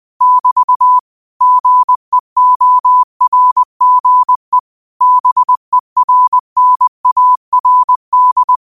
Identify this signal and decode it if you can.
Morse code, heard as a keyed tone, about 20 wpm, 1000 Hz. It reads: <BT> GEORGE BERNARD
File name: qotdfiles / 20wpm